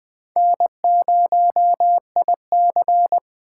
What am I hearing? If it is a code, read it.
Morse code, heard as a keyed tone, about 21 wpm, 700 Hz. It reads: N0IC